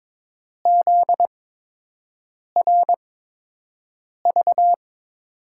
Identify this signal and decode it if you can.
Morse code, heard as a keyed tone, about 22 wpm, spaced as for 6 wpm, 700 Hz. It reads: ZRV
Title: USA Call Signs - 6 Characters - 8x Character Spacing - 22wpm